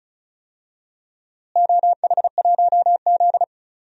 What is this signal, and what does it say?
Morse code, heard as a keyed tone, about 34 wpm, 700 Hz. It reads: OH1Z
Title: DXCC-100 Call Signs - 4 Characters - Repeated 2x - 35wpm